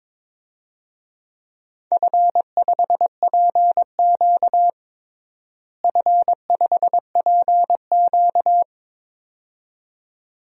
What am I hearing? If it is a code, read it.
Morse code, heard as a keyed tone, about 22 wpm, 700 Hz. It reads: F5PQ F5PQ
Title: DXCC-100 Call Signs - 4 Characters - Repeated 2x - 22wpm